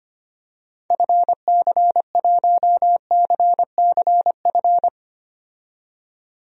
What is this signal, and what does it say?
Morse code, heard as a keyed tone, about 25 wpm, 700 Hz. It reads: FC1CCF
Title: DXCC-100 Call Signs - 6 Characters - 25wpm